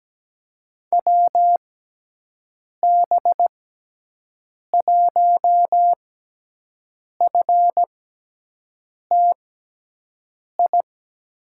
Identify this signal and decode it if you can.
Morse code, heard as a keyed tone, about 17 wpm, spaced as for 6 wpm, 700 Hz. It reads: WB1FTI
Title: USA Call Signs - 6 Characters - 6x Character Spacing - 17wpm